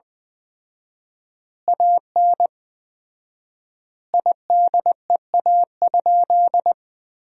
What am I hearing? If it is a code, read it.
Morse code, heard as a keyed tone, about 20 wpm, 700 Hz. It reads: AN IDEA?
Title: 4x Word-Spacing - Sentences from Top 400 Words - 20wpm